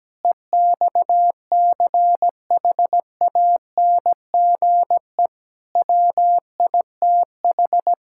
Morse code, heard as a keyed tone, about 17 wpm, 700 Hz. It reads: EXCHANGE WITH
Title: Sentences from Top 1000 Words - Review Part 6 - 17wpm